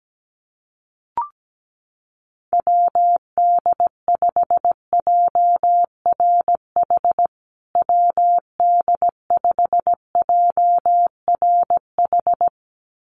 Morse code, heard as a keyed tone, about 17 wpm, 700 Hz. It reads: WD5JRH WD5JRH
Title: USA Call Signs Repeated 2x 17wpm